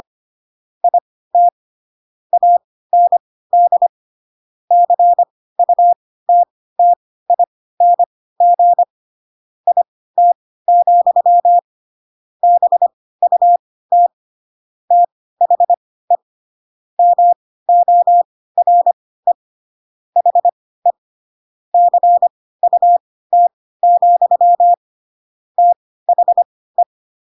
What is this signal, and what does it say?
Morse code, heard as a keyed tone, about 25 wpm, spaced as for 16 wpm, 700 Hz. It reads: IT AND CUTTING IT, BUT THE MORE HE CUT, THE
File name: Pinocchio / Pinocchio ch3 10 25 ik1hge